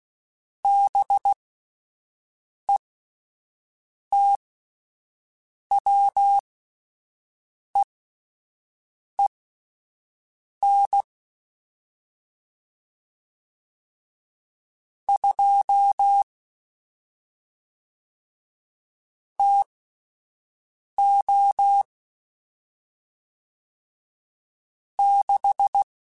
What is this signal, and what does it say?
Morse code, heard as a keyed tone, about 16 wpm, spaced as for 5 wpm, 800 Hz. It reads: BETWEEN 2 TO 6
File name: Spaced Characters / Spaced Characters-mp3s / Weather 4 10 sp @10